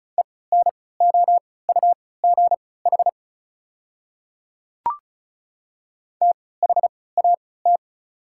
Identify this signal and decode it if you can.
Morse code, heard as a keyed tone, about 34 wpm, spaced as for 20 wpm, 700 Hz. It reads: ENOUGH THAT
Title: 3x Character Spacing - Top 500 Words - 35wpm